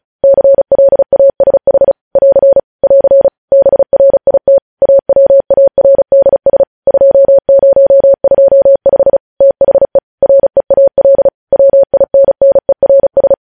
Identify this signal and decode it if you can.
Morse code, heard as a keyed tone, about 36 wpm, 550 Hz. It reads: CLASH <AR> <AR> BRIT AWARDS 2025 THE REAL WINNERS
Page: News Headlines in Morse Code at 35 WPM